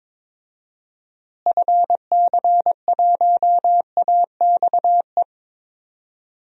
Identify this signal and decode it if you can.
Morse code, heard as a keyed tone, about 22 wpm, 700 Hz. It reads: FC1AXE
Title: DXCC-100 Call Signs - 6 Characters - Repeated 3x - 22wpm